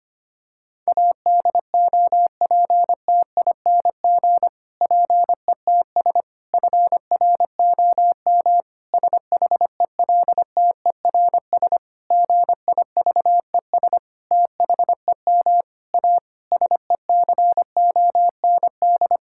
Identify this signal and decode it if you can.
Morse code, heard as a keyed tone, about 25 wpm, 700 Hz. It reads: ADOPTING PETS FROM SHELTERS GIVES THEM A SECOND
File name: practice / 25wpm / Difficult Sentences with Hints - 25wpm